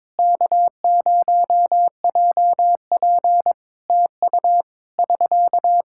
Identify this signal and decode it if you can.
Morse code, heard as a keyed tone, about 21 wpm, 700 Hz. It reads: K0JP TU <SK>
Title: Practice QSO - Exchange 4 - Ending - 22wpm